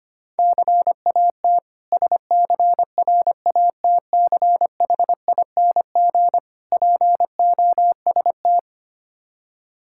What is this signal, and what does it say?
Morse code, heard as a keyed tone, about 25 wpm, 700 Hz. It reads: CAT SCRATCHING POST
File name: practice / 25wpm / Difficult Words with Hints - 25wpm